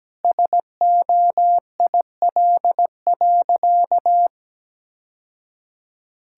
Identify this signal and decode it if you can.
Morse code, heard as a keyed tone, about 17 wpm, 700 Hz. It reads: SOIL.
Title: Sentences from Top 1000 Words - Review Part 5 - 17wpm